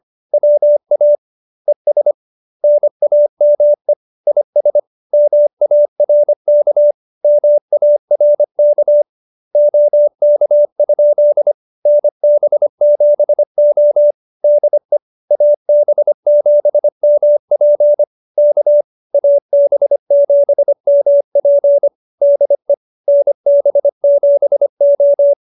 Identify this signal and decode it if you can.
Morse code, heard as a keyed tone, about 25 wpm, 590 Hz. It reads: WA ES NAME IS MARK MARK OK? NB7O DE AB7MP K AB7MP DE NB7O
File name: QSO306 25